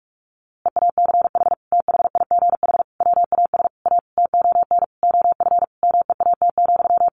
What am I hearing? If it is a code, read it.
Morse code, heard as a keyed tone, about 46 wpm, 700 Hz. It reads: EACH THIGH WAS A TON OF MEAT,
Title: A Sound Of Thunder 45wpm